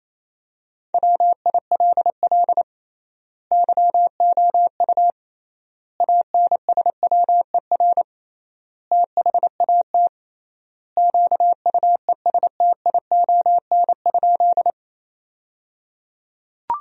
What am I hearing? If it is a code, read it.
Morse code, heard as a keyed tone, about 28 wpm, 700 Hz. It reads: WILL YOU ANSWER THAT QUESTION?
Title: 3x Word-Spacing - Sentences from Top 400 Words - 28wpm